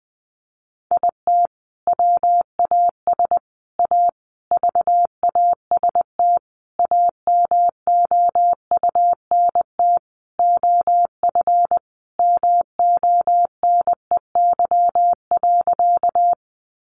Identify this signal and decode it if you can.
Morse code, heard as a keyed tone, about 19 wpm, 700 Hz. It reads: IT WAS A VAST AMOUNT OF MONEY.
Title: Sentences from Top 1300 Words 20wpm